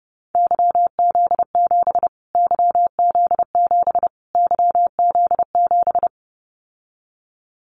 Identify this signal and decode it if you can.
Morse code, heard as a keyed tone, about 30 wpm, 700 Hz. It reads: YZ7 YZ7 YZ7
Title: Callsign Prefixes for Top 25 DXCC - Repeated 3x - 30wpm